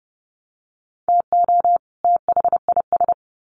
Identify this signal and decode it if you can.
Morse code, heard as a keyed tone, about 28 wpm, 700 Hz. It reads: TO THIS
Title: Sets of 2 Words 30wpm